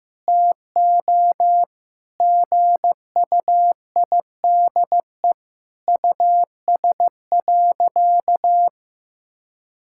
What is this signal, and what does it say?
Morse code, heard as a keyed tone, about 16 wpm, 700 Hz. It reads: TO GUIDE US.
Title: Sentences from Top 1000 Words - Review Part 5 - 15wpm